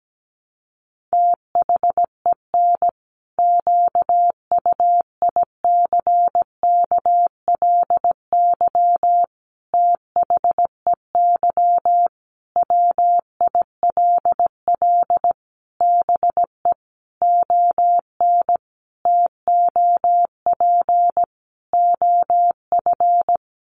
Morse code, heard as a keyed tone, about 17 wpm, 700 Hz. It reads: THEN QUICKLY THEY WILL BE ON TOP OF
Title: Sentences from Top 700 Words 17wpm